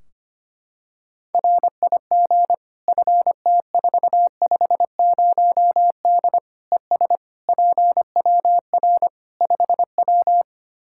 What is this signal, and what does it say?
Morse code, heard as a keyed tone, about 25 wpm, 700 Hz. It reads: RIG FT450D ES PWR 5W
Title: Practice QSO - Exchange 2 - Info - 25wpm